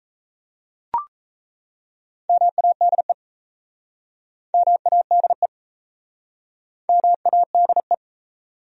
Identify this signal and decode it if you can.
Morse code, heard as a keyed tone, about 37 wpm, 700 Hz. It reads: MADE MADE MADE